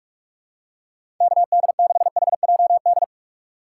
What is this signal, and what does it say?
Morse code, heard as a keyed tone, about 41 wpm, 700 Hz. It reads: K D 6 H J D
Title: USA Call Signs - 6 Characters - 6x Character Spacing - 45wpm